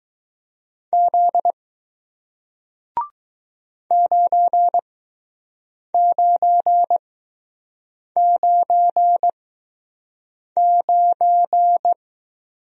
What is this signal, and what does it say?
Morse code, heard as a keyed tone, about 20 wpm, spaced as for 6 wpm, 700 Hz. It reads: Z 9999